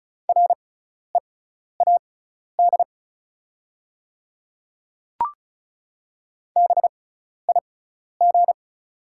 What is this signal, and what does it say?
Morse code, heard as a keyed tone, about 36 wpm, spaced as for 12 wpm, 700 Hz. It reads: READ BIG
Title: 6x Character Spacing - Top 200 Words - 35wpm